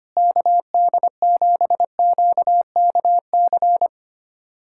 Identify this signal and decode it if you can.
Morse code, heard as a keyed tone, about 25 wpm, 700 Hz. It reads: KD7QKC
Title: USA Callsign DE Callsign 25wpm